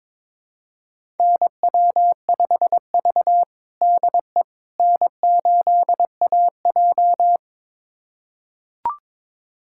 Morse code, heard as a keyed tone, about 22 wpm, 700 Hz. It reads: NW5V DE N8AJ
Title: USA Callsign DE Callsign - 4 Characters - 22wpm